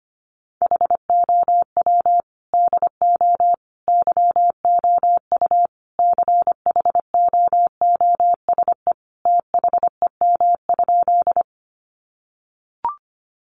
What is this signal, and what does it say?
Morse code, heard as a keyed tone, about 25 wpm, 700 Hz. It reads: HOW DO YOU CHOOSE THEM?
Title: Sentences from Top 700 Words 25wpm